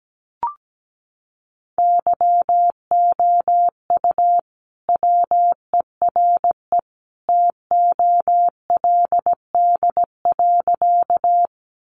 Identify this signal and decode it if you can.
Morse code, heard as a keyed tone, about 17 wpm, 700 Hz. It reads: YOU WERE TOLD.
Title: Sentences from Top 400 Words 17wpm